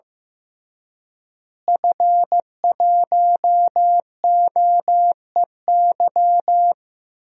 Morse code, heard as a keyed tone, about 15 wpm, 700 Hz. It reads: F1OEY
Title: DXCC-100 Call Signs - 5 Characters - Repeated 3x - 15wpm